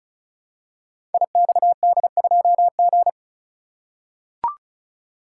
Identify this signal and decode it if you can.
Morse code, heard as a keyed tone, about 36 wpm, 700 Hz. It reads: IXD2G
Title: Code Groups - 5 Letter:Numbers - 35wpm